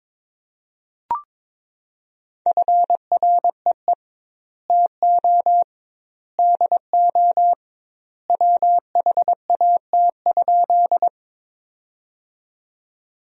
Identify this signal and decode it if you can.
Morse code, heard as a keyed tone, about 22 wpm, 700 Hz. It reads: FREE TO DO WHAT?
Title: 2x Word-Spacing - Sentences from Top 500 Words - 22wpm